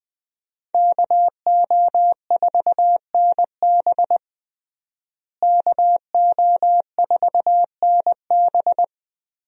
Morse code, heard as a keyed tone, about 19 wpm, 700 Hz. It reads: KO4NB KO4NB
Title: USA Call Signs - 5 Characters - Repeated 3x - 20wpm